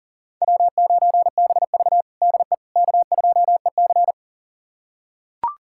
Morse code, heard as a keyed tone, about 37 wpm, 700 Hz. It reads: W9BV DE K2EC
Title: USA Callsign DE Callsign - 4 Characters - 40wpm